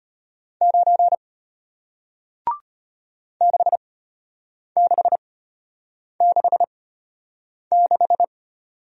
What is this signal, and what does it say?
Morse code, heard as a keyed tone, about 33 wpm, spaced as for 8 wpm, 700 Hz. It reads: 9 6666